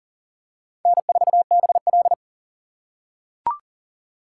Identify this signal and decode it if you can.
Morse code, heard as a keyed tone, about 42 wpm, 700 Hz. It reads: N4BL
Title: USA Call Signs - 4 Characters - 8x Character Spacing - 40wpm